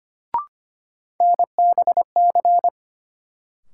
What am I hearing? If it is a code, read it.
Morse code, heard as a keyed tone, about 25 wpm, 700 Hz. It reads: NBC
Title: Sets of 5 Words Encore - New Word Drill - 25wpm